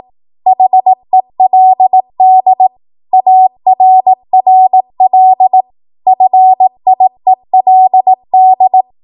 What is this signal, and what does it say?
Morse code, heard as a keyed tone, about 18 wpm, 750 Hz. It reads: HELD ARRL FIELD
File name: Archive / 18 WPM / 18WPM